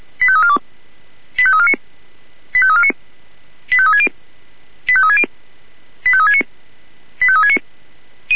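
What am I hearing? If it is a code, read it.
Five-tone selective call (ZVEI), heard as quick runs of five tones, tones of about 70 ms. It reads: 85342 85349 85349 85360 85360 85369 85369
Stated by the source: ZVEI five-tone selective call 85340, received by SDR.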